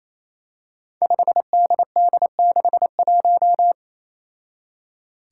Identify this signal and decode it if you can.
Morse code, heard as a keyed tone, about 28 wpm, 700 Hz. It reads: 5DD61
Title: Five Challenging Letter-Number Combinations 28wpm